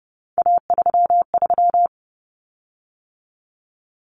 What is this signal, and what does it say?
Morse code, heard as a keyed tone, about 30 wpm, 700 Hz. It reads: A33
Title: Common Cut Numbers - 3-Digit 30wpm